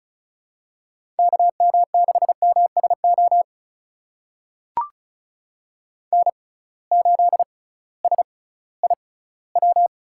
Morse code, heard as a keyed tone, about 34 wpm, spaced as for 19 wpm, 700 Hz. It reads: KM6MSO N8SIW
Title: USA Call Signs - 6x Character Spacing - 35wpm